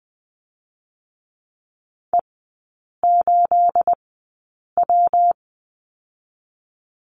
Morse code, heard as a keyed tone, about 20 wpm, 700 Hz. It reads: E 8 W
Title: 212 Lesson 34e - Number 8 - Three Characters 20wpm